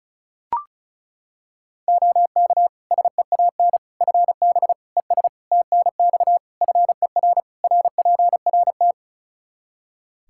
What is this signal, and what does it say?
Morse code, heard as a keyed tone, about 36 wpm, 700 Hz. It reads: OK SEAN FB ES TNX FER RPRT
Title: Practice QSO - Exchange 2 - Info - 35wpm